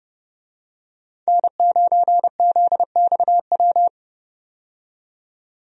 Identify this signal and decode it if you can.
Morse code, heard as a keyed tone, about 30 wpm, 700 Hz. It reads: N9ZXW
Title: USA Call Signs - 5 Characters - 30wpm